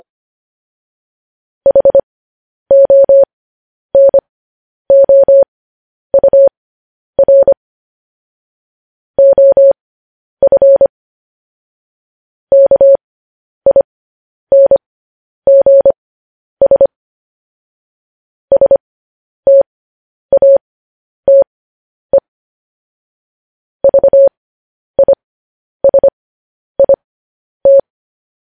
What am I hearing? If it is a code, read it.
Morse code, heard as a keyed tone, about 25 wpm, spaced as for 10 wpm, 550 Hz. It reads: HONOUR OF KINGS STATE VISIT
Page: news headlines in morse code at 10 wpm